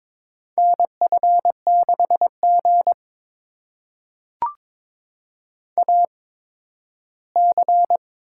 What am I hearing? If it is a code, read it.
Morse code, heard as a keyed tone, about 23 wpm, spaced as for 13 wpm, 700 Hz. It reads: NF6G AC
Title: USA Call Signs - 4 Characters - 8x Character Spacing - 22wpm